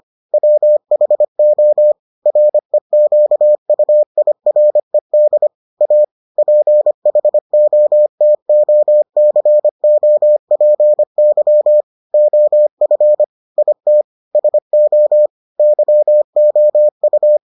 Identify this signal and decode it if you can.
Morse code, heard as a keyed tone, about 25 wpm, 600 Hz. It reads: WHO REQUIRED A PHOTOCOPY OF IT SO YOU